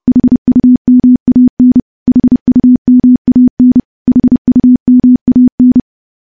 Morse code, heard as a keyed tone, about 32 wpm, 250 Hz. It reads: HUMAN HUMAN HUMAN
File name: morse / morse-training-30WPM